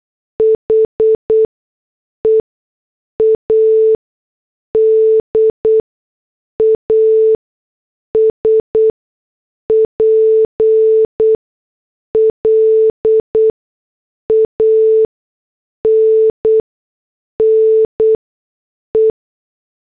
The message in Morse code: HEAD AS PLANNE